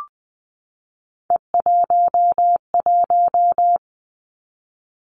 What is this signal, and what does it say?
Morse code, heard as a keyed tone, about 20 wpm, 700 Hz. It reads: E11
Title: Three Challenging Letter-Number Combinations 20wpm